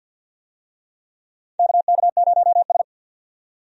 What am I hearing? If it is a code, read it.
Morse code, heard as a keyed tone, about 47 wpm, 700 Hz. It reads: KK0S
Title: USA Call Signs - 4 Characters - 50wpm